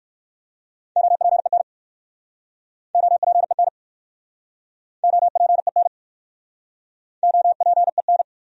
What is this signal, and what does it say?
Morse code, heard as a keyed tone, about 65 wpm, 700 Hz. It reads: OPEN OPEN OPEN OPEN